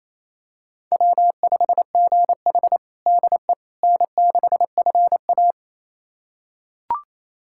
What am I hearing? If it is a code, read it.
Morse code, heard as a keyed tone, about 28 wpm, 700 Hz. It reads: W5GH DE N6FA
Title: USA Callsign DE Callsign - 4 Characters - 28wpm